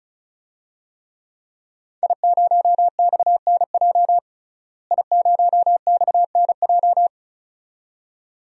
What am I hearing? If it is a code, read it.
Morse code, heard as a keyed tone, about 36 wpm, 700 Hz. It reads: I0XNJ I0XNJ
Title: DXCC-100 Call Signs - 5 Characters - Repeated 2x - 35wpm